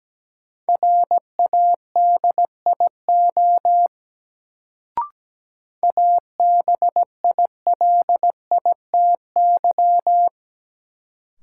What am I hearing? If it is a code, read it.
Morse code, heard as a keyed tone, about 17 wpm, 700 Hz. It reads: RADIO ABILITY
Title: Sentences from Top 1000 Words - Review Part 3 - New Word Drill - 17wpm